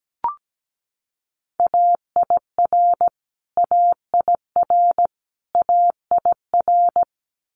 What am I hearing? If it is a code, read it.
Morse code, heard as a keyed tone, about 17 wpm, 700 Hz. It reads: AIR AIR AIR
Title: Top 500 Words Repeated 17wpm